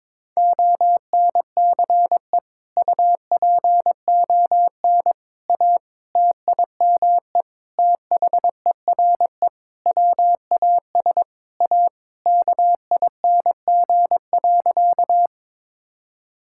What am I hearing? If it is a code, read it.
Morse code, heard as a keyed tone, about 22 wpm, 700 Hz. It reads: ONCE UPON A TIME THERE WAS A KING.
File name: Sentences with 7-Word Phrases - 22wpm